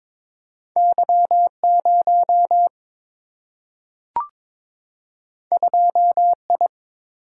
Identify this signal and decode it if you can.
Morse code, heard as a keyed tone, about 22 wpm, 700 Hz. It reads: Y 0 2 I
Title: Single-Letter Single-Number 22wpm